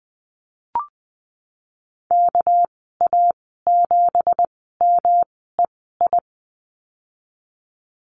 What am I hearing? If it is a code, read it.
Morse code, heard as a keyed tone, about 20 wpm, 700 Hz. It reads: K A 7 M E I
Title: 238 Lesson 38g - Letter q - Call Signs 20wpm